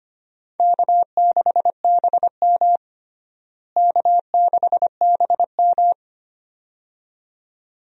Morse code, heard as a keyed tone, about 25 wpm, 700 Hz. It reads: K6BM K6BM
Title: USA Call Signs - 4 Characters - Repeated 2x - 25wpm